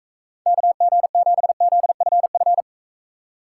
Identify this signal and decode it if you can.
Morse code, heard as a keyed tone, about 42 wpm, 700 Hz. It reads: KG7ZFF